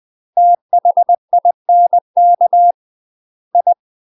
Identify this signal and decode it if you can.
Morse code, heard as a keyed tone, about 20 wpm, 700 Hz. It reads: THINK I
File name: UK MASTER QSOs TRAINING / 20WPM / UK MASTER QSOs 0066